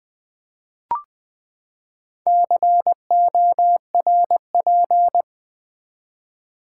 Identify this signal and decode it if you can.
Morse code, heard as a keyed tone, about 20 wpm, 700 Hz. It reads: CORP
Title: Build A Word - 4 Letters At A Time - 11-Letter Words - 20wpm